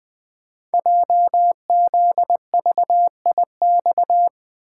Morse code, heard as a keyed tone, about 20 wpm, 700 Hz. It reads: JZVIX
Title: Code Groups - 5 Letters - 20wpm